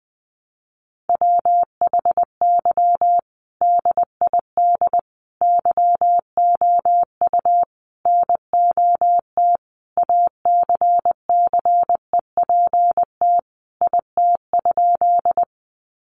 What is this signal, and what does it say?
Morse code, heard as a keyed tone, about 20 wpm, 700 Hz. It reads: WHY DID YOU NOT ACCEPT IT?
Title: Sentences from Top 1300 Words 20wpm